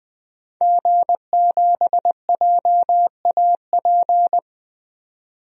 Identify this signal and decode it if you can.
Morse code, heard as a keyed tone, about 20 wpm, 700 Hz. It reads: G7JAP
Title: DXCC-100 Call Signs - 5 Characters - 20wpm